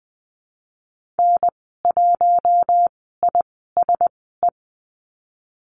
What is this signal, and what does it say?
Morse code, heard as a keyed tone, about 20 wpm, spaced as for 15 wpm, 700 Hz. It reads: N1ISE
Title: 062 Lesson 11g - Letter h - Call Signs 20wpm